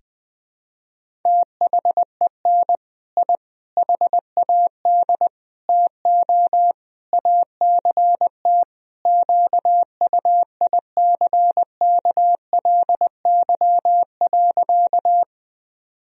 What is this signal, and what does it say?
Morse code, heard as a keyed tone, about 20 wpm, 700 Hz. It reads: THEN I HAD TO ACT QUICKLY.
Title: Sentences from Top 1000 Words - Review Part 1 - 20wpm